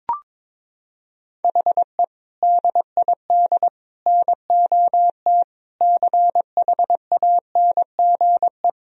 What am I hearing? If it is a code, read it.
Morse code, heard as a keyed tone, about 22 wpm, 700 Hz. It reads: HE DID NOT CHANGE
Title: Sentences from Top 1000 Words - Review Part 1 - 22wpm